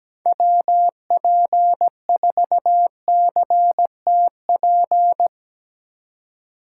Morse code, heard as a keyed tone, about 17 wpm, 700 Hz. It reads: WP4CTP
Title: USA Call Signs - 3x Character Spacing - 17wpm